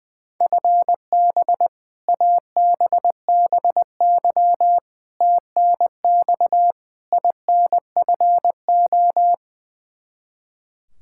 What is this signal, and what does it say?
Morse code, heard as a keyed tone, about 21 wpm, 700 Hz. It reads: FB ABBY TNX INFO
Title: Practice QSO - Exchange 3 - More Info - 20wpm